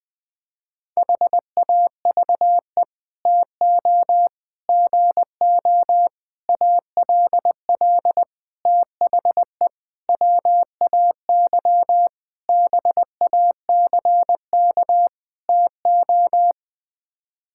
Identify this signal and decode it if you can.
Morse code, heard as a keyed tone, about 19 wpm, 700 Hz. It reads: HAVE TO GO ALL THE WAY BACK TO
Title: Sets of 8 Words - 20wpm